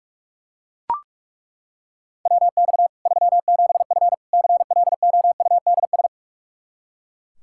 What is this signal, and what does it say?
Morse code, heard as a keyed tone, about 41 wpm, 700 Hz. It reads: WX 37F CLOUDS
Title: Practice QSO - Exchange 2 - Info - 45wpm